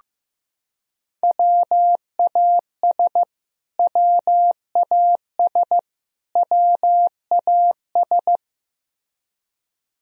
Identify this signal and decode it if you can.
Morse code, heard as a keyed tone, about 15 wpm, 700 Hz. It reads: WAS WAS WAS
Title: Top 100 Words Repeated 15wpm